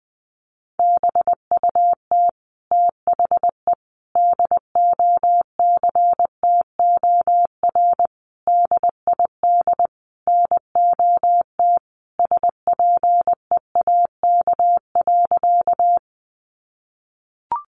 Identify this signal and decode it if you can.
Morse code, heard as a keyed tone, about 20 wpm, 700 Hz. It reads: BUT THE DOCTOR DID NOT SPEAK.
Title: Sentences from Top 1500 Words 20wpm